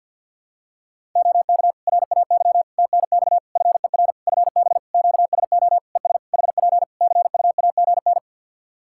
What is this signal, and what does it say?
Morse code, heard as a keyed tone, about 50 wpm, 700 Hz. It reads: OK RAY TNX FER FB QSO ES HP CUAGN
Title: Practice QSO - Exchange 4 - Ending - 50wpm